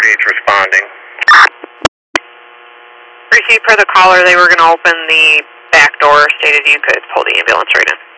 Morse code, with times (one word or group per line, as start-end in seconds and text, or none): none